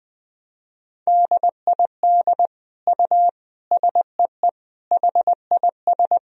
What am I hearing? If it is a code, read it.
Morse code, heard as a keyed tone, about 20 wpm, 700 Hz. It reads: DID U SEE HIS
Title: Sentences from Top 300 Words and Abbreviations 20wpm